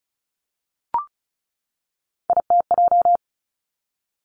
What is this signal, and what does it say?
Morse code, heard as a keyed tone, about 36 wpm, 700 Hz. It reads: ITJ
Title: Three Challenging Letter-Number Combinations 35wpm